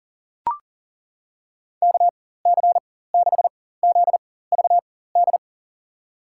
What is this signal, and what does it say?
Morse code, heard as a keyed tone, about 40 wpm, spaced as for 19 wpm, 700 Hz. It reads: KC6ZVD
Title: USA Call Signs - 6 Characters - 4x Character Spacing - 40wpm